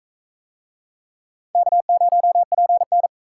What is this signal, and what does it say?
Morse code, heard as a keyed tone, about 43 wpm, 700 Hz. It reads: K0PN